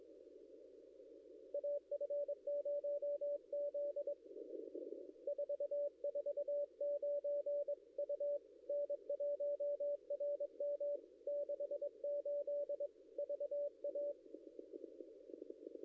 Morse code, heard as a keyed tone, about 22 wpm, 580 Hz. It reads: AF0Z 449 U N1RM 68 VA
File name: contests / ss-cw-k5tr-op-n5rz -0700